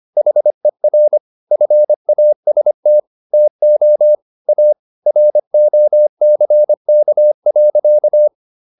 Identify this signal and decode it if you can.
Morse code, heard as a keyed tone, about 25 wpm, 600 Hz. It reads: HER FAST TO A ROCK.